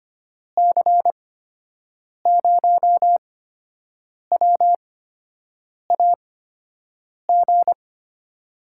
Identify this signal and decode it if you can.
Morse code, heard as a keyed tone, about 25 wpm, spaced as for 7 wpm, 700 Hz. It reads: C0WAG
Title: USA Call Signs - 8x Character Spacing - 25wpm